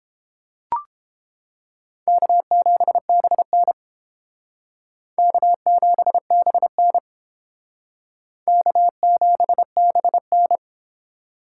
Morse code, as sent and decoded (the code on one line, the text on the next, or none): -.- --... -... -. / -.- --... -... -. / -.- --... -... -.
K7BN K7BN K7BN